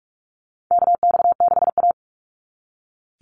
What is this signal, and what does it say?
Morse code, heard as a keyed tone, about 46 wpm, 700 Hz. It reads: KX6A
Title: NA SOTA Chaser Callsigns 45wpm